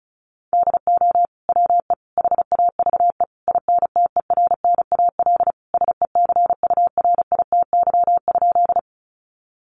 Morse code, heard as a keyed tone, about 34 wpm, 700 Hz. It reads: DO WE HAVE INTERNAL SECURITY?
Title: Sentences from Top 1400 Words 35wpm